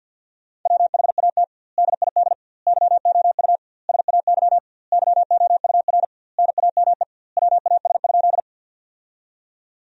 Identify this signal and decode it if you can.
Morse code, heard as a keyed tone, about 50 wpm, 700 Hz. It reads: WHAT DID YOU SAY YOUR NAME WAS?
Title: 2x Word-Spacing - Sentences from Top 200 Words - 50wpm